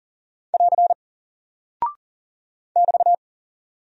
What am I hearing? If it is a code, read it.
Morse code, heard as a keyed tone, about 40 wpm, 700 Hz. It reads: <AR> <BT>
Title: Single Character (Letter:Number:Punctuation:Prosign) 40wpm